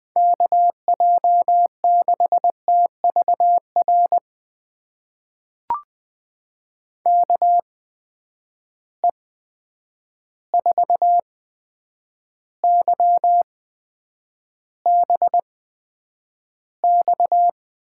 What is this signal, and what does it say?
Morse code, heard as a keyed tone, about 20 wpm, spaced as for 9 wpm, 700 Hz. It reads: KJ6TVR KE4YBX
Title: USA Call Signs - 6 Characters - 8x Character Spacing - 20wpm